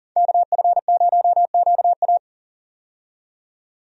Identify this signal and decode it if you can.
Morse code, heard as a keyed tone, about 40 wpm, 700 Hz. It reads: KF0QA
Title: USA Call Signs - 5 Characters - 40wpm